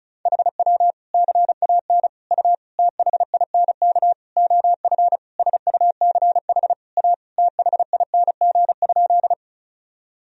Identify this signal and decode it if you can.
Morse code, heard as a keyed tone, about 34 wpm, 700 Hz. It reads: HW CAN U THINK OF SUCH A THING?
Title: Sentences from Top 200 Words and Abbreviations 35wpm